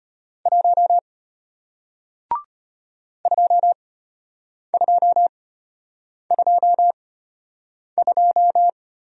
Morse code, heard as a keyed tone, about 33 wpm, spaced as for 8 wpm, 700 Hz. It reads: 1 2222